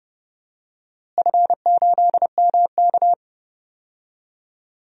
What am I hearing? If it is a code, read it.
Morse code, heard as a keyed tone, about 30 wpm, 700 Hz. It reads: F8MK
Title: DXCC-100 Call Signs - 4 Characters - Repeated 2x - 30wpm